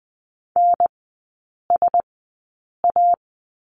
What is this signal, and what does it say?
Morse code, heard as a keyed tone, about 20 wpm, 700 Hz. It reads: N S A
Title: 035 Lesson 7e - Letter s - Three Characters 20wpm